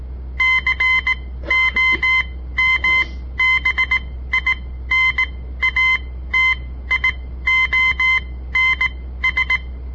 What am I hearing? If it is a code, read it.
Morse code, heard as a keyed tone, about 17 wpm, spaced as for 13 wpm, 2050 Hz. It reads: COMBINATIONS
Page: On the air Morse code practice from the Southern Montana Amatuer Radio Association; all code, no read back